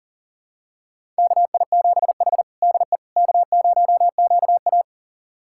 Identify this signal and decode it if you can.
Morse code, heard as a keyed tone, about 40 wpm, 700 Hz. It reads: KI7H DE K0QA
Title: USA Callsign DE Callsign - 4 Characters - 40wpm